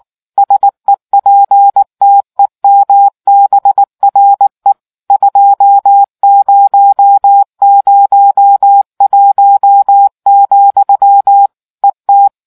Morse code, heard as a keyed tone, about 19 wpm, 800 Hz. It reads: SEPTEMBRE 2001, ET